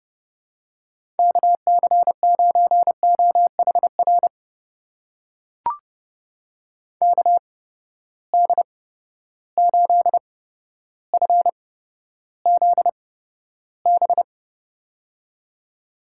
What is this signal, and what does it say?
Morse code, heard as a keyed tone, about 30 wpm, spaced as for 13 wpm, 700 Hz. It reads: KC9OHR KD8FZB
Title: USA Call Signs - 6 Characters - 8x Character Spacing - 30wpm